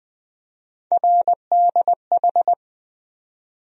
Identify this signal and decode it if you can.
Morse code, heard as a keyed tone, about 20 wpm, 700 Hz. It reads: RDH
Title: USA Call Sign Postfixes - 3 Letters 20wpm